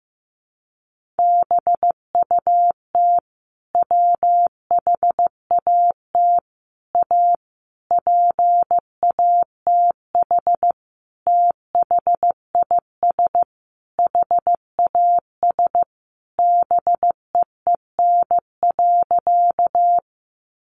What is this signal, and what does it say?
Morse code, heard as a keyed tone, about 15 wpm, 700 Hz. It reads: BUT WHAT A PATH THIS HAS BEEN.
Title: Sentences from Top 1400 Words 15wpm